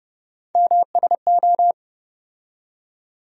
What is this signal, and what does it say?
Morse code, heard as a keyed tone, about 30 wpm, 700 Hz. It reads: MSO
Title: USA Call Sign Postfixes - 3 Letters 30wpm